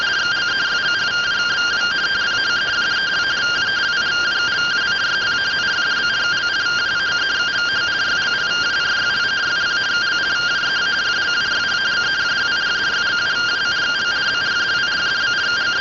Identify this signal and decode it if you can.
Two-tone teleprinter signal, about 170 Hz shift.